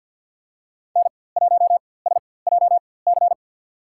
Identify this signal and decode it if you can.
Morse code, heard as a keyed tone, about 46 wpm, spaced as for 23 wpm, 700 Hz. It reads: N1SJC
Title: USA Call Signs - 5 Characters - 4x Character Spacing - 50wpm